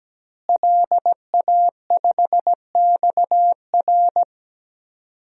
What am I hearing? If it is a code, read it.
Morse code, heard as a keyed tone, about 17 wpm, 700 Hz. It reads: LA5XR
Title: DXCC-100 Call Signs - 5 Characters - Repeated 3x - 17wpm